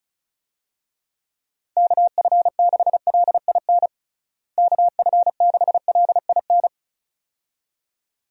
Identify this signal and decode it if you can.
Morse code, heard as a keyed tone, about 34 wpm, 700 Hz. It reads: KF6LIN KF6LIN
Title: USA Call Signs - 6 Characters - Repeated 2x - 35wpm